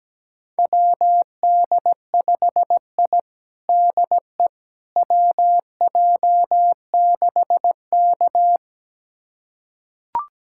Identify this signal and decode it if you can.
Morse code, heard as a keyed tone, about 17 wpm, 700 Hz. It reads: WD5I DE WJ6K
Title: USA Callsign DE Callsign - 4 Characters - 17wpm